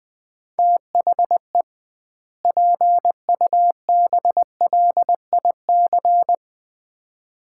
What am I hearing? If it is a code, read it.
Morse code, heard as a keyed tone, about 21 wpm, 700 Hz. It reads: THE PUBLIC
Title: 2x Word-Spacing - Sets of 2 Words - 20wpm